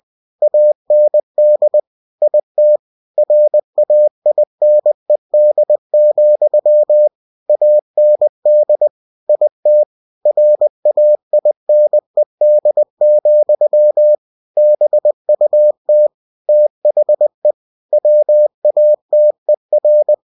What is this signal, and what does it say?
Morse code, heard as a keyed tone, about 20 wpm, 600 Hz. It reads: AND IT RAINED, AND IT RAINED, BUT THE WATER
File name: cw-books / winnie-the-pooh / winnie-20wpm-page-0095